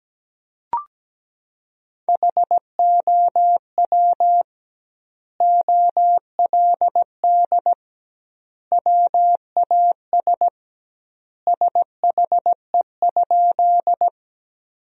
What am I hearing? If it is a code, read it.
Morse code, heard as a keyed tone, about 17 wpm, 700 Hz. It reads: HOW OLD WAS SHE?
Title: 2x Word-Spacing - Sentences from Top 200 Words - 17wpm